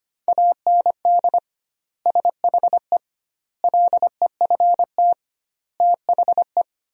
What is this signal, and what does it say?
Morse code, heard as a keyed tone, about 25 wpm, 700 Hz. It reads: AND SHE LEFT THE
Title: 2x Word-Spacing - Sentences from Top 300 Words - 25wpm